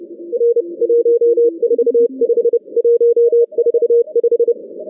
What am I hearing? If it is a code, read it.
Morse code, heard as a keyed tone, about 31 wpm, 470 Hz. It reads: R 145 145
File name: 2015 cq ww cw / 2015 cqwww k5tr - 0-18